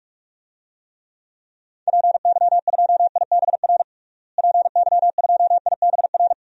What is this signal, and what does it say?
Morse code, heard as a keyed tone, about 47 wpm, 700 Hz. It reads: PY2IBR PY2IBR
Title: DXCC-100 Call Signs - 6 Characters - Repeated 2x - 45wpm